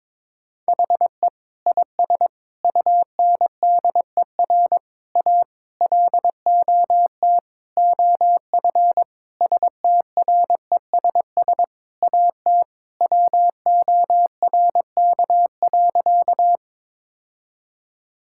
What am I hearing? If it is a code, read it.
Morse code, heard as a keyed tone, about 22 wpm, 700 Hz. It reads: HE IS UNDER A LOT OF STRESS AT WORK.
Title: Sentences from Top 1000 Words - Review Part 6 - 22wpm